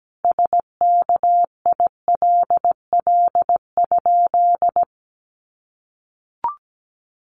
Morse code, heard as a keyed tone, about 17 wpm, 700 Hz. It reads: SKILL?
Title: Sentences from Top 1200 Words 17wpm